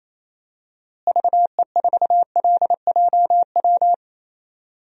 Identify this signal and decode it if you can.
Morse code, heard as a keyed tone, about 28 wpm, 700 Hz. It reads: VE4LJW
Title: DXCC-100 Call Signs - 6 Characters - Repeated 2x - 28wpm